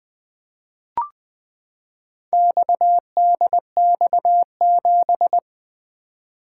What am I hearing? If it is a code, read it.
Morse code, heard as a keyed tone, about 20 wpm, 700 Hz. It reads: XDX7
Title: Four Letter-Number Combinations - 20wpm